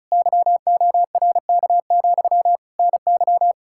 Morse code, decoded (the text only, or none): YORK, NY